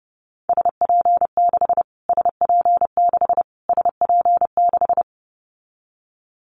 Morse code, heard as a keyed tone, about 30 wpm, 690 Hz. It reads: SP6 SP6 SP6
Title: Callsign Prefixes for Top 25 DXCC - Repeated 3x - 30wpm